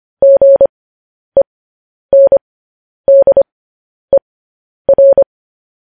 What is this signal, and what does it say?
Morse code, heard as a keyed tone, about 25 wpm, spaced as for 10 wpm, 550 Hz. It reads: GENDER
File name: podcast / News headlines-1x-25x10